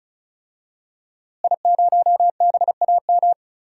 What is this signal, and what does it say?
Morse code, heard as a keyed tone, about 36 wpm, 700 Hz. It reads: I0BAM
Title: DXCC-100 Call Signs - 5 Characters - Repeated 3x - 35wpm